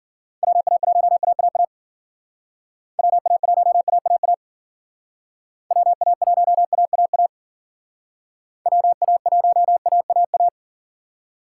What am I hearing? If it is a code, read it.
Morse code, heard as a keyed tone, about 48 wpm, 700 Hz. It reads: WA1AAA WA1AAA WA1AAA WA1AAA
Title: Speed Racing - USA 2x3 Call Signs 40wpm